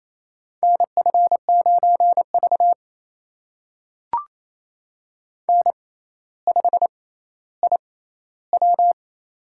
Morse code, heard as a keyed tone, about 28 wpm, spaced as for 14 wpm, 700 Hz. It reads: NF9V N5IW
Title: USA Call Signs - 4 Characters - 6x Character Spacing - 28wpm